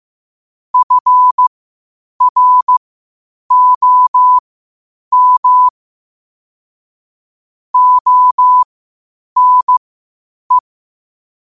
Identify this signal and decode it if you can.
Morse code, heard as a keyed tone, about 15 wpm, spaced as for 8 wpm, 1000 Hz. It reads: FROM ONE
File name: qotdfiles / 8wpm